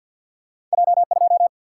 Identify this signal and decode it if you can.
Morse code, heard as a keyed tone, about 47 wpm, 700 Hz. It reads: J 2
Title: Single-Letter Single-Number 50wpm